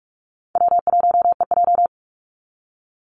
Morse code, heard as a keyed tone, about 44 wpm, 700 Hz. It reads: W1EJ
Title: NA SOTA Activator Callsigns 45wpm